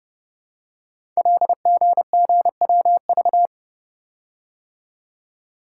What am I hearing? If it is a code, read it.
Morse code, heard as a keyed tone, about 30 wpm, 700 Hz. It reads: LGGWV
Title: Code Groups - 5 Letters - 30wpm